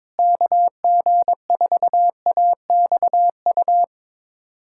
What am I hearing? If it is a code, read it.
Morse code, heard as a keyed tone, about 22 wpm, 700 Hz. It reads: KG4AXU
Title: USA Callsign DE Callsign - 6 Characters - 22wpm